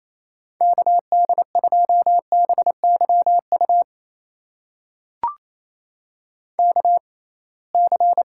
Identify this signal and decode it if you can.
Morse code, heard as a keyed tone, about 28 wpm, spaced as for 21 wpm, 700 Hz. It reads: KD2BYU KC
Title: USA Call Signs - 6 Characters - 6x Character Spacing - 28wpm